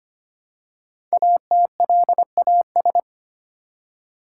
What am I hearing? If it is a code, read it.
Morse code, heard as a keyed tone, about 25 wpm, 690 Hz. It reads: ATLAS
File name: practice / 25wpm / Easy Words with Hints - 25wpm